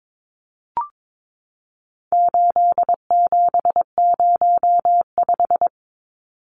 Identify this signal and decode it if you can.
Morse code, heard as a keyed tone, about 22 wpm, 700 Hz. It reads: 8705
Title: Numbers - 4-Digit 22wpm